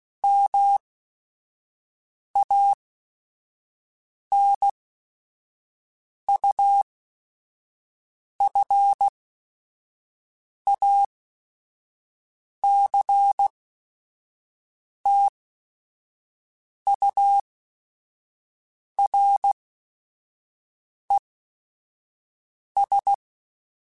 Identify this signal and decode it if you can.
Morse code, heard as a keyed tone, about 16 wpm, spaced as for 5 wpm, 800 Hz. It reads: MANUFACTURES
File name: Chuckles / Chuckles-mp3s / Guaranteed @5